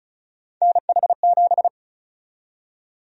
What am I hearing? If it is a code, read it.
Morse code, heard as a keyed tone, about 36 wpm, 700 Hz. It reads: NH7
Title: USA Call Sign Prefixes - 35wpm